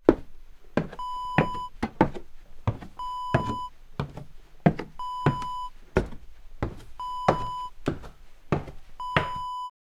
There is no code.